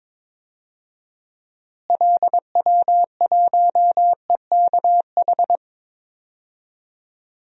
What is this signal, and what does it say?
Morse code, heard as a keyed tone, about 22 wpm, 700 Hz. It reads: LW1EKH
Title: DXCC-100 Call Signs - 6 Characters - Repeated 2x - 22wpm